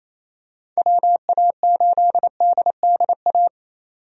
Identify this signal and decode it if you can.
Morse code, heard as a keyed tone, about 28 wpm, 700 Hz. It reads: WA8DDA
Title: USA Call Signs - 6 Characters - 28wpm